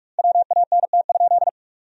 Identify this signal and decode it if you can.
Morse code, heard as a keyed tone, about 46 wpm, 700 Hz. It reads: WANT?
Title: 2x Word-Spacing - Sentences from Top 200 Words - 45wpm